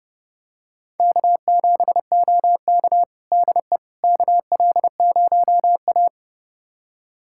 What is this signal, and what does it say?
Morse code, heard as a keyed tone, about 30 wpm, 700 Hz. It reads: K7OK DE KL0A
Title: USA Callsign DE Callsign - 4 Characters - 30wpm